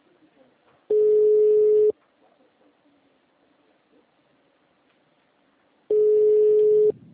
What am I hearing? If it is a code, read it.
Ringback tone (Europe, 425 Hz), with speech.